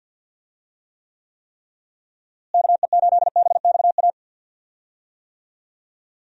Morse code, heard as a keyed tone, about 50 wpm, 700 Hz. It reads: KE8BXA
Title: USA Call Signs - 6 Characters - Repeated 3x - 50wpm